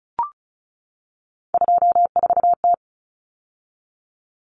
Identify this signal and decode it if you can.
Morse code, heard as a keyed tone, about 34 wpm, 700 Hz. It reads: 24T
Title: Common Cut Numbers - 3-Digit 35wpm